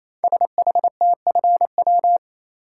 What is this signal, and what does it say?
Morse code, heard as a keyed tone, about 27 wpm, 700 Hz. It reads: SHTFW
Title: Code Groups - 5 Letters - 28wpm